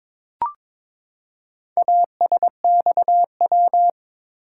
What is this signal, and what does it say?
Morse code, heard as a keyed tone, about 22 wpm, 700 Hz. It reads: ASXW
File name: practice / 22wpm / Four Letter-Number Combinations - 22wpm